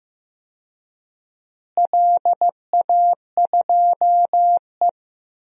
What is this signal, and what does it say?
Morse code, heard as a keyed tone, about 15 wpm, 700 Hz. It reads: LA2E
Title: DXCC-100 Call Signs - 4 Characters - Repeated 3x - 15wpm